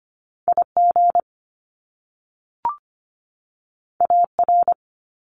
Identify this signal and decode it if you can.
Morse code, heard as a keyed tone, about 25 wpm, 700 Hz. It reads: I G A R
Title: Top Two Letter Combinations in Words 25wpm